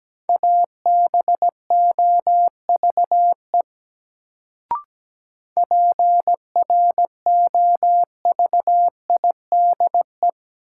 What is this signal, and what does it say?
Morse code, heard as a keyed tone, about 17 wpm, 700 Hz. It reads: ABOVE PROVIDE
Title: Sentences from Top 1000 Words - Review Part 2 - New Word Drill - 17wpm